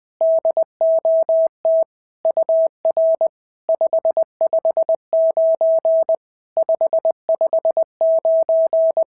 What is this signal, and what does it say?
Morse code, heard as a keyed tone, about 20 wpm, 690 Hz. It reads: DOT UR 559 559
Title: SOTA QSO Practice 20wpm